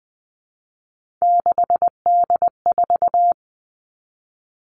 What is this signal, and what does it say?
Morse code, heard as a keyed tone, about 20 wpm, 700 Hz. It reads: 6D4
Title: Three Challenging Letter-Number Combinations 20wpm